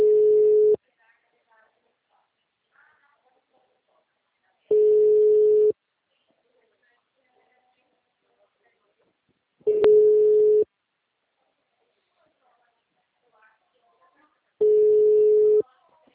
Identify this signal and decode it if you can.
Ringback tone (Europe, 425 Hz), with speech.